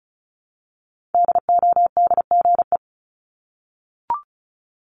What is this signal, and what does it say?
Morse code, heard as a keyed tone, about 36 wpm, 700 Hz. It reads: DODGE
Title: Multi-letter Phonemes Repeated 3x 35wpm